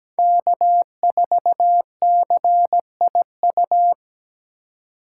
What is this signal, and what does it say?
Morse code, heard as a keyed tone, about 20 wpm, 700 Hz. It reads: K4CIU
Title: Speed Racing - USA 1x3 Call Signs 17wpm